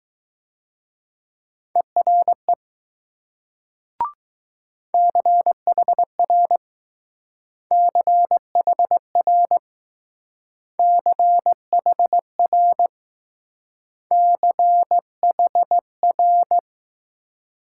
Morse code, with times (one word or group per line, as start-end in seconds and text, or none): 1.8-2.5 ERE
4.9-6.6 CHR
7.7-9.6 CHR
10.8-12.9 CHR
14.1-16.6 CHR